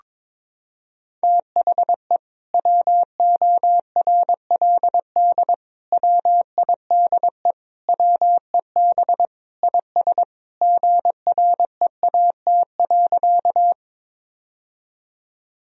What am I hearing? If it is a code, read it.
Morse code, heard as a keyed tone, about 22 wpm, 700 Hz. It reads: THE WORLD WIDE WEB IS GREAT.
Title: Sentences from Top 1000 Words - Review Part 4 - 22wpm